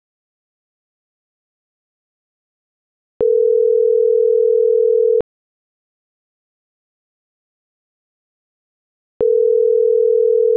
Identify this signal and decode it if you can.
Ringback tone (North America), 440 + 480 Hz.